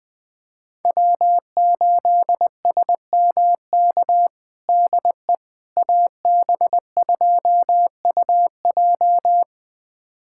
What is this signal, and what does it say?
Morse code, heard as a keyed tone, about 20 wpm, 700 Hz. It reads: W8SMK DE AB2UJ
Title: USA Callsign DE Callsign - 5 Characters - 20wpm